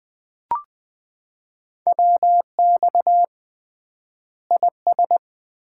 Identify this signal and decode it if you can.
Morse code, heard as a keyed tone, about 21 wpm, 700 Hz. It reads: WX IS
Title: Common QSO Phrases - 3x Word-Spacing - 20wpm